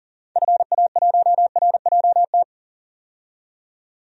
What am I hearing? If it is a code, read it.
Morse code, heard as a keyed tone, about 37 wpm, 700 Hz. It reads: FA1RJT
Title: DXCC-100 Call Signs - 6 Characters - 40wpm